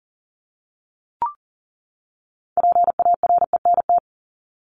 Morse code, heard as a keyed tone, about 40 wpm, 700 Hz. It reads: PARENT
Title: ES:S Suffix - Top 100 Words with Base Words 40wpm